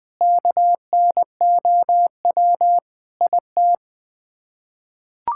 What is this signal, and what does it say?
Morse code, heard as a keyed tone, about 21 wpm, 700 Hz. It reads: KNOW IT
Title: Sets of 9 Words - 20wpm